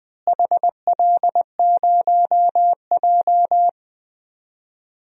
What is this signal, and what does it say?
Morse code, heard as a keyed tone, about 21 wpm, 700 Hz. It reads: HL0J
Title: DXCC-100 Call Signs - 4 Characters - Repeated 3x - 20wpm